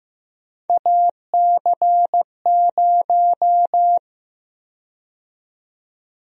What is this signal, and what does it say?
Morse code, heard as a keyed tone, about 15 wpm, 700 Hz. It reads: AC0
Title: USA Call Sign Prefixes - 15wpm